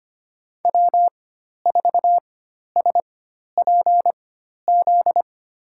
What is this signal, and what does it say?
Morse code, heard as a keyed tone, about 25 wpm, spaced as for 12 wpm, 700 Hz. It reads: W4SPZ
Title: USA Call Signs - 4x Character Spacing - 25wpm